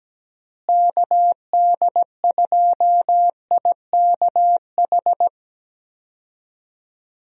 Morse code, heard as a keyed tone, about 17 wpm, 700 Hz. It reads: KD2IKH
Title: USA Call Signs - 6 Characters - Repeated 2x - 17wpm